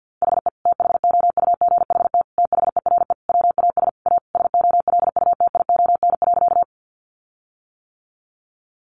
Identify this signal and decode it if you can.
Morse code, heard as a keyed tone, about 48 wpm, 700 Hz. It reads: HE THOUGHT THERE WAS A SOLUTION.
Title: Sentences from Top 1300 Words 50wpm